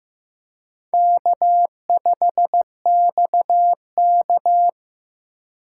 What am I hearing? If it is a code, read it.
Morse code, heard as a keyed tone, about 15 wpm, 700 Hz. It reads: K5XK
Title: K1USN SST QSO Practice - 15wpm